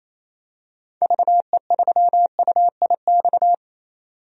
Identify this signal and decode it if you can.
Morse code, heard as a keyed tone, about 28 wpm, 700 Hz. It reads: VE3UIX
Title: DXCC-100 Call Signs - 6 Characters - 28wpm